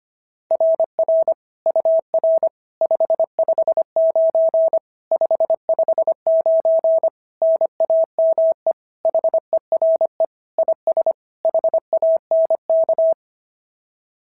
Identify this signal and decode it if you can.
Morse code, heard as a keyed tone, about 25 wpm, 660 Hz. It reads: RR UR 559 559 NAME HERE IS HANK
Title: SOTA QSO Practice 25wpm